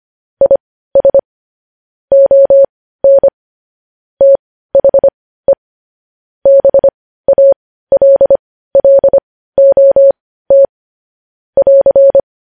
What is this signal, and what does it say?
Morse code, heard as a keyed tone, about 25 wpm, spaced as for 15 wpm, 550 Hz. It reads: IS ON THE BALLOT <AR>
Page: News headlines at 15